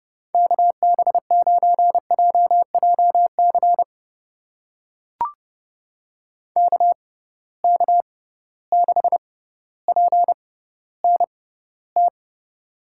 Morse code, heard as a keyed tone, about 30 wpm, spaced as for 15 wpm, 700 Hz. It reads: KB9JJC KK6PNT
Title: USA Call Signs - 6 Characters - 6x Character Spacing - 30wpm